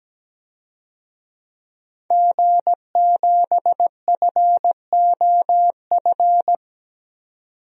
Morse code, heard as a keyed tone, about 17 wpm, 700 Hz. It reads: G7FOF
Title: DXCC-100 Call Signs - 5 Characters - 17wpm